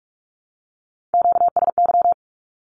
Morse code, heard as a keyed tone, about 46 wpm, 700 Hz. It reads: QSY
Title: Q-Codes Repeated 3x 45wpm